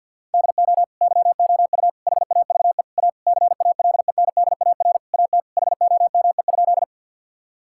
Morse code, heard as a keyed tone, about 50 wpm, 700 Hz. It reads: DO YOU HAVE A CALENDAR AT HOME?
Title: Sentences from Top 1600 Words 50wpm